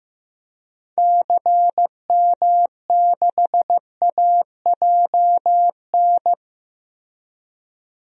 Morse code, heard as a keyed tone, about 15 wpm, 700 Hz. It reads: CM6AJN
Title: DXCC-100 Call Signs - 6 Characters - Repeated 3x - 15wpm